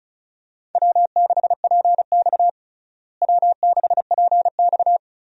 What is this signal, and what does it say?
Morse code, heard as a keyed tone, about 36 wpm, 700 Hz. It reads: W6PX W6PX
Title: USA Call Signs - 4 Characters - Repeated 3x - 35wpm